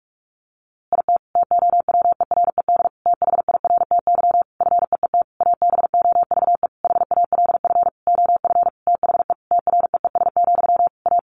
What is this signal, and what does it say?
Morse code, heard as a keyed tone, about 44 wpm, 700 Hz. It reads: IT TOWERED THIRTY FEET ABOVE HALF OF THE TREES, A
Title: A Sound Of Thunder 45wpm